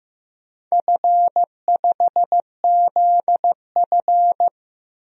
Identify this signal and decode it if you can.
Morse code, heard as a keyed tone, about 15 wpm, 700 Hz. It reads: F5ZF
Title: DXCC-100 Call Signs - 4 Characters - 15wpm